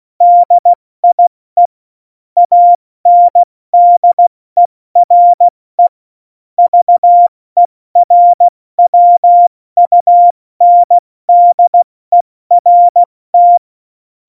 Morse code, heard as a keyed tone, about 16 wpm, 700 Hz. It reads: DIE ANDERE VERWUNDERT
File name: Morsekurs / Witze / Witz 22. Mai 2018, 70 F80